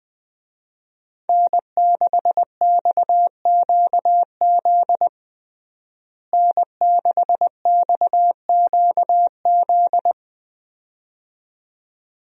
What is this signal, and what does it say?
Morse code, heard as a keyed tone, about 20 wpm, 700 Hz. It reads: N6XQZ N6XQZ
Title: USA Call Signs - 5 Characters - Repeated 2x - 20wpm